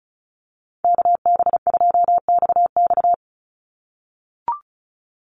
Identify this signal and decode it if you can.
Morse code, heard as a keyed tone, about 34 wpm, 700 Hz. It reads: KB2XX
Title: NA SOTA Chaser Callsigns 35wpm